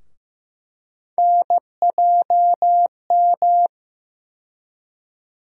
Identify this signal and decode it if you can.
Morse code, heard as a keyed tone, about 15 wpm, 700 Hz. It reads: NJM
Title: USA Call Sign Postfixes - 3 Letters 15wpm